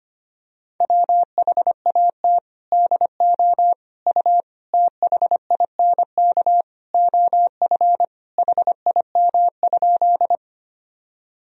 Morse code, heard as a keyed tone, about 25 wpm, 700 Hz. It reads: WHAT DO U THINK OF HIM?
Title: Sentences from Top 200 Words and Abbreviations 25wpm